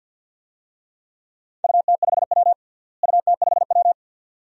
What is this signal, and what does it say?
Morse code, heard as a keyed tone, about 50 wpm, 700 Hz. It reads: UT5W UT5W
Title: DXCC-100 Call Signs - 4 Characters - Repeated 2x - 50wpm